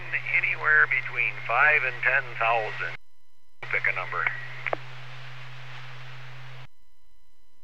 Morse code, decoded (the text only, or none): none